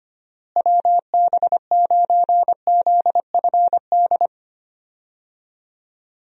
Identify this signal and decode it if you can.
Morse code, heard as a keyed tone, about 25 wpm, 700 Hz. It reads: WB9ZFD
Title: USA Call Signs - 6 Characters - Repeated 2x - 25wpm